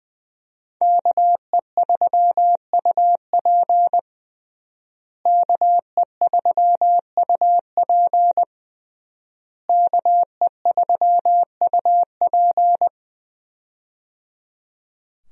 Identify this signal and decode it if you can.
Morse code, heard as a keyed tone, about 20 wpm, 700 Hz. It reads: KE3UP KE3UP KE3UP
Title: USA Call Signs - 5 Characters - Repeated 3x - 20wpm